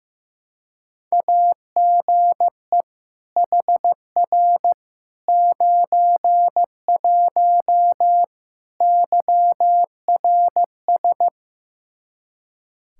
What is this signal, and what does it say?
Morse code, heard as a keyed tone, about 16 wpm, 700 Hz. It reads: AGE HR 91 YRS
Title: K1USN SST QSO Practice 15wpm